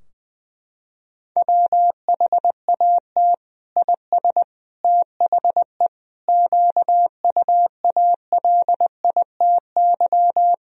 Morse code, heard as a keyed tone, about 20 wpm, 700 Hz. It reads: WHAT IS THE QUALITY
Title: Sentences from Top 1000 Words - Review Part 4 - 20wpm